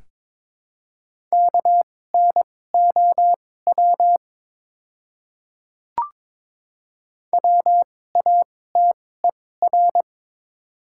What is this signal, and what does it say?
Morse code, heard as a keyed tone, about 22 wpm, spaced as for 16 wpm, 690 Hz. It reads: KNOW WATER
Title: 2x Character Spacing - Top 100 Words - 22wpm